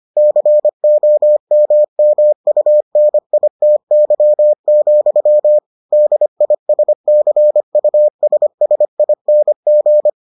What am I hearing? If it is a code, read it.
Morse code, heard as a keyed tone, about 25 wpm, 600 Hz. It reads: COMMUNITY, DISCUSSING